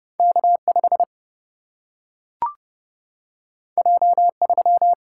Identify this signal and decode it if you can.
Morse code, heard as a keyed tone, about 30 wpm, 700 Hz. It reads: K5 J3
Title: Single-Letter Single-Number 30wpm